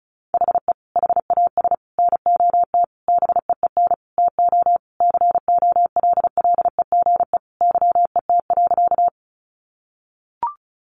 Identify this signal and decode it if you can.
Morse code, heard as a keyed tone, about 34 wpm, 700 Hz. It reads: HE HAS NOT BEEN TO COLLEGE YET.
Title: Sentences from Top 1300 Words 35wpm